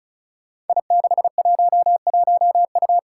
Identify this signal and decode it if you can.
Morse code, heard as a keyed tone, about 36 wpm, 700 Hz. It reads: I611U
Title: Five Challenging Letter-Number Combinations 35wpm